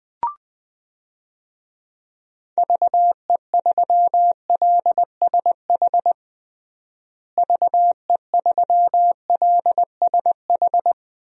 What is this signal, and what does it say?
Morse code, heard as a keyed tone, about 20 wpm, 700 Hz. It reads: VE3LSH VE3LSH
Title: DXCC-100 Call Signs - 6 Characters - Repeated 3x - 20wpm